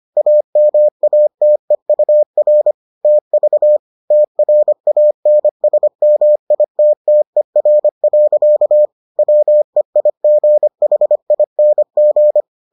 Morse code, heard as a keyed tone, about 25 wpm, 600 Hz. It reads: AMATEUR TV TRANSMITTER. WEIGHING